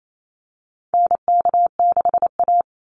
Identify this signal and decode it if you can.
Morse code, heard as a keyed tone, about 28 wpm, 700 Hz. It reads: NK6A
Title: NA SOTA Chaser Callsigns 28wpm